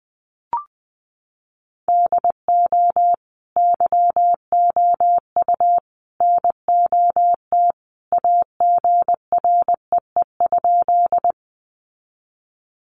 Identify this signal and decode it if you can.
Morse code, heard as a keyed tone, about 20 wpm, 700 Hz. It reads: DO YOU NOT AGREE?
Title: Sentences from Top 1100 Words 20wpm